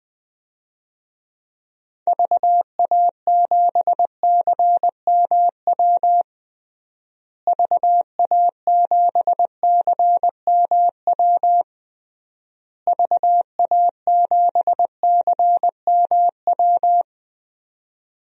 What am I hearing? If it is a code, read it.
Morse code, heard as a keyed tone, about 20 wpm, 700 Hz. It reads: VA7CMW VA7CMW VA7CMW
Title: DXCC-100 Call Signs - 6 Characters - Repeated 3x - 20wpm